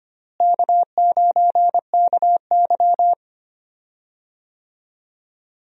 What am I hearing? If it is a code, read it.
Morse code, heard as a keyed tone, about 25 wpm, 700 Hz. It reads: K9KY
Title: USA Call Signs - 4 Characters - Repeated 2x - 25wpm